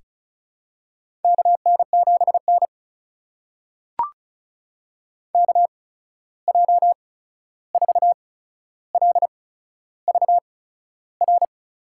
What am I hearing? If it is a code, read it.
Morse code, heard as a keyed tone, about 36 wpm, spaced as for 13 wpm, 700 Hz. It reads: KN7N KJ4LVR
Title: USA Call Signs - 8x Character Spacing - 35wpm